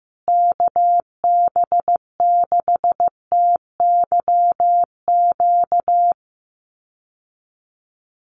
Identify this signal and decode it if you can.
Morse code, heard as a keyed tone, about 15 wpm, 700 Hz. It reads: KB6TYQ
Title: USA Call Signs Repeated 2x 15wpm